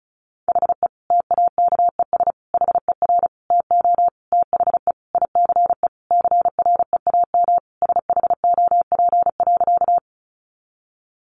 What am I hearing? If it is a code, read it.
Morse code, heard as a keyed tone, about 34 wpm, 700 Hz. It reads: HE TAKES HER TO THE ICE CREAM SHOP.
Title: Additional Easy Phrases 35wpm